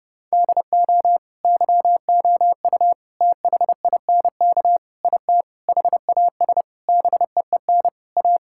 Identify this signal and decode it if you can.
Morse code, heard as a keyed tone, about 30 wpm, 700 Hz. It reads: DO YOU THINK IT HAS BEEN A
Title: Sentences from Top 1000 Words - Review Part 6 - 30wpm